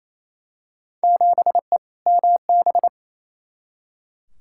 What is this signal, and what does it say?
Morse code, heard as a keyed tone, about 28 wpm, 700 Hz. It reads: 7E MB
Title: Field Day Exchanges Encore - 28wpm